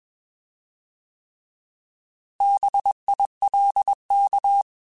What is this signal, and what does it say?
Morse code, heard as a keyed tone, about 21 wpm, 800 Hz. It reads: BILK
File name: Alphabetical / Alphabetical mp3s / Bs @20